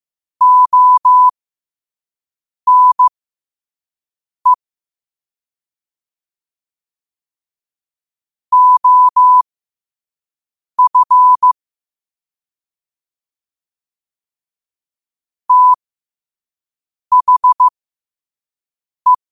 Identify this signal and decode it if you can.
Morse code, heard as a keyed tone, about 15 wpm, spaced as for 5 wpm, 1000 Hz. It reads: ONE OF THE
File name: qotdfiles / 5wpm